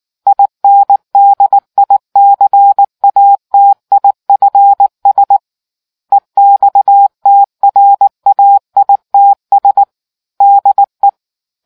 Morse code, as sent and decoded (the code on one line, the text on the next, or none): .. -. -.. .. -.-. .- - .. ..-. ... / . -..- - .-. .- .. - ... / -.. .
INDICATIFS EXTRAITS DE